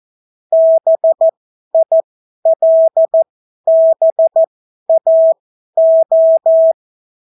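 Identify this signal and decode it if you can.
Morse code, heard as a keyed tone, about 14 wpm, spaced as for 11 wpm, 650 Hz. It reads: BILBAO
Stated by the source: Example Spanish CW QSO from a Morse learning book, 14 wpm characters with 8 wpm spacing.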